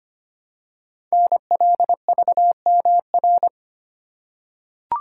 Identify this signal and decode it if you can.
Morse code, heard as a keyed tone, about 25 wpm, 700 Hz. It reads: NLVMR
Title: Code Groups - 5 Letters - 25wpm